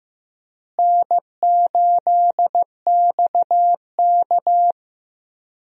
Morse code, heard as a keyed tone, about 15 wpm, 700 Hz. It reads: N8XK
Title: USA Call Signs - 4 Characters - Repeated 2x - 15wpm